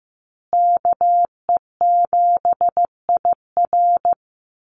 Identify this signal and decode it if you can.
Morse code, heard as a keyed tone, about 15 wpm, 700 Hz. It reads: KE7IR
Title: USA Call Signs Repeated 2x 15wpm